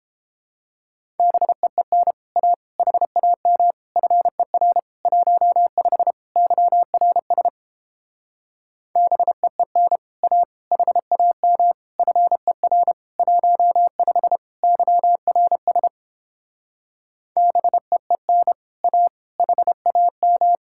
Morse code, heard as a keyed tone, about 29 wpm, 700 Hz. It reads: BEEN A HAM FER 15 YRS BEEN A HAM FER 15 YRS BEEN A HAM